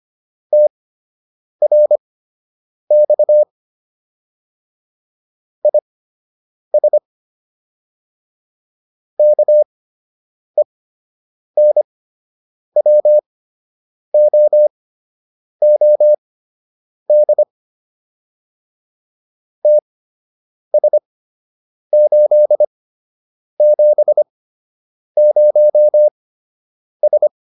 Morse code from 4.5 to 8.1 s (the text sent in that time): IS